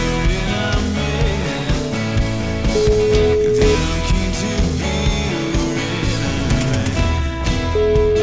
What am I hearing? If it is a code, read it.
Ringback tone (Europe, 425 Hz), with noise.